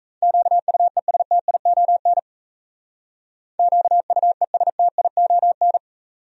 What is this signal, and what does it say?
Morse code, heard as a keyed tone, about 34 wpm, 700 Hz. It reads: QUESTION QUESTION
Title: Speed-Racing Top 500 Words 28wpm